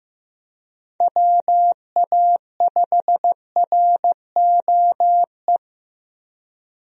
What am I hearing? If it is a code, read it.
Morse code, heard as a keyed tone, about 16 wpm, 700 Hz. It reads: WA5ROE
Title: USA Call Signs - 6 Characters - Repeated 3x - 15wpm